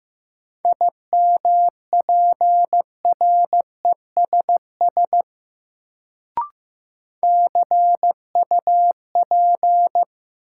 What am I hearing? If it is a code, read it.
Morse code, heard as a keyed tone, about 15 wpm, 700 Hz. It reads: IMPRESS CUP
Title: Sentences from Top 1600 Words - New Word Drill Alt - 15wpm